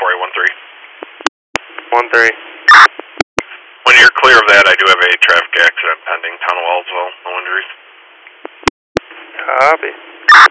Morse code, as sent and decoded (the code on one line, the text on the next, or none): none